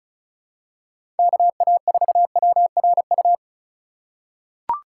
Morse code, heard as a keyed tone, about 34 wpm, 700 Hz. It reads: KA4WRU
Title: USA Call Signs - 6 Characters - 4x Character Spacing - 35wpm